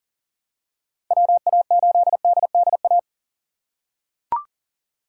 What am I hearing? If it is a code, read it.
Morse code, heard as a keyed tone, about 40 wpm, 700 Hz. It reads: WA8DDA
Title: USA Call Signs - 6 Characters - 2x Character Spacing - 40wpm